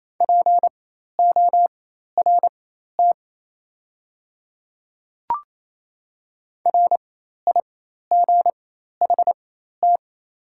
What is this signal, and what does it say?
Morse code, heard as a keyed tone, about 28 wpm, spaced as for 13 wpm, 700 Hz. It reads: PORT RIGHT
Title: 4x Character Spacing - Top 300 Words - 28wpm